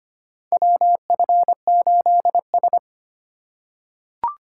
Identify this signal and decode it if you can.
Morse code, heard as a keyed tone, about 25 wpm, 700 Hz. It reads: WF8S
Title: USA Call Signs - 4 Characters - 3x Character Spacing - 25wpm